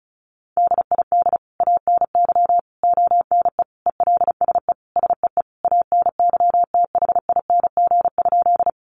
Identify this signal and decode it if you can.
Morse code, heard as a keyed tone, about 34 wpm, 700 Hz. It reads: DID ANY ONE ELSE SEE ANYTHING?
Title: Sentences from Top 1000 Words 35wpm